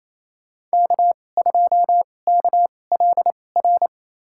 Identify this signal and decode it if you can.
Morse code, heard as a keyed tone, about 28 wpm, spaced as for 20 wpm, 700 Hz. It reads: K2KLR
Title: USA Call Signs - 2x Character Spacing - 28wpm